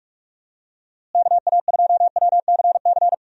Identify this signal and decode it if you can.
Morse code, heard as a keyed tone, about 46 wpm, 700 Hz. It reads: KA2WCC
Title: USA Call Signs - 6 Characters - Repeated 3x - 45wpm